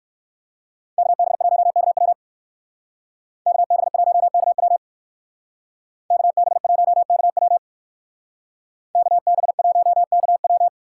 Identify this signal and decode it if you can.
Morse code, heard as a keyed tone, about 58 wpm, 700 Hz. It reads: KB1KW KB1KW KB1KW KB1KW